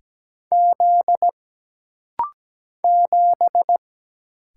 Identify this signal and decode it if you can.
Morse code, heard as a keyed tone, about 17 wpm, 700 Hz. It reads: Z 7
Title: Single Letter-Number - Rapid-Fire 17wpm